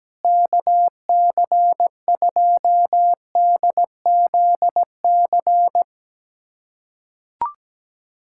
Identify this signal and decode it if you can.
Morse code, heard as a keyed tone, about 17 wpm, 700 Hz. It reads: KC2DZC
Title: USA Callsign DE Callsign 17wpm